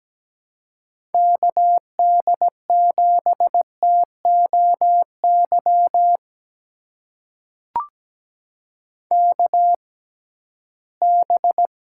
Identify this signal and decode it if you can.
Morse code, heard as a keyed tone, about 17 wpm, spaced as for 13 wpm, 700 Hz. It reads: KD7TOY KB
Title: USA Call Signs - 6 Characters - 6x Character Spacing - 17wpm